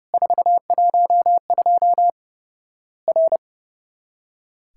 Morse code, heard as a keyed tone, about 30 wpm, 700 Hz. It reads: 412 R
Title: Worked All Europe DX Contest - QTC Exchange - 30wpm